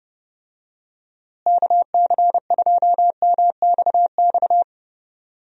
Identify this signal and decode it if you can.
Morse code, heard as a keyed tone, about 30 wpm, 700 Hz. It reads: KC2MXX
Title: USA Call Signs - 6 Characters - Repeated 3x - 30wpm